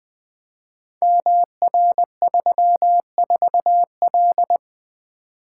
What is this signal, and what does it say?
Morse code, heard as a keyed tone, about 21 wpm, 700 Hz. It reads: MR34L
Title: Code Groups - 5 Letter:Numbers - 20wpm